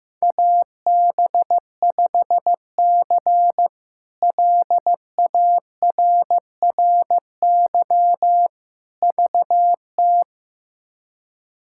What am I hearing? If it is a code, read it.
Morse code, heard as a keyed tone, about 15 wpm, 700 Hz. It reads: AB5C LARRY VT
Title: K1USN SST QSO Practice - 15wpm